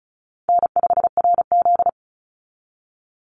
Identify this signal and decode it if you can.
Morse code, heard as a keyed tone, about 34 wpm, 700 Hz. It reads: N5RZ
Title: Easy Call Signs Repeated 2x - 35wpm